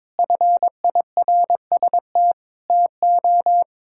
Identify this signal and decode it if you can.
Morse code, heard as a keyed tone, about 22 wpm, 700 Hz. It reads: FIRST TO
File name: practice / 22wpm / Sentences with 2-Word Phrases - 22wpm